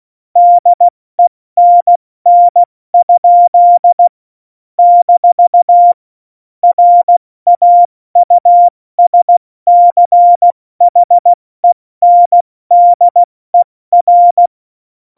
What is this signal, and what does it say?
Morse code, heard as a keyed tone, about 16 wpm, spaced as for 14 wpm, 700 Hz. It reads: DENN? - RAUSCHENDER
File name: Morsekurs / Witze / Witz 05. September 17, Tempo 70